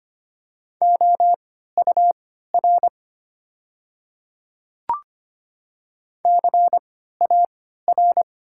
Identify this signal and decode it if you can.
Morse code, heard as a keyed tone, about 25 wpm, spaced as for 14 wpm, 700 Hz. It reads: OUR CAR
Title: 3x Character Spacing - Top 300 Words - 25wpm